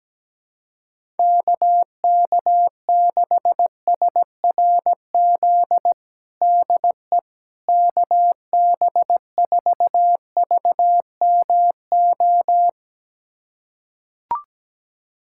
Morse code, heard as a keyed tone, about 17 wpm, 700 Hz. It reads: KK6SRZ DE KB4VMO
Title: USA Callsign DE Callsign - 6 Characters - 17wpm